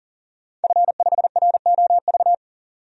Morse code, heard as a keyed tone, about 40 wpm, 700 Hz. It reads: F5ROV
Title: DXCC-100 Call Signs - 5 Characters - 40wpm